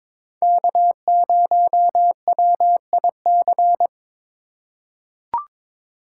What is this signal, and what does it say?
Morse code, heard as a keyed tone, about 22 wpm, 700 Hz. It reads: K0WIC
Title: USA Call Signs - 5 Characters - 4x Character Spacing - 22wpm